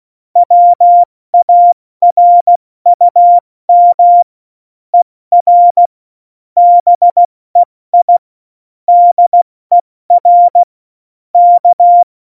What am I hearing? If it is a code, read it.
Morse code, heard as a keyed tone, about 16 wpm, 700 Hz. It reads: WARUM ER BEI DER K
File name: Morsekurs / Witze / Witz 21 Juli 2020, 70 F80